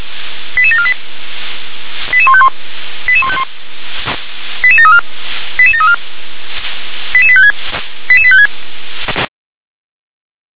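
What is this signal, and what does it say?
Five-tone selective call (ZVEI), heard as quick runs of five tones, tones of about 80 ms. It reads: 88748 88151 88151 88734 88734 88856 88856
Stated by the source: ZVEI five-tone selective call received on SDR.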